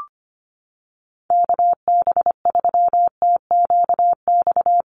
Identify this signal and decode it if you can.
Morse code, heard as a keyed tone, about 25 wpm, 700 Hz. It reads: KB3TQX
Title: USA Call Signs Repeated 2x 25wpm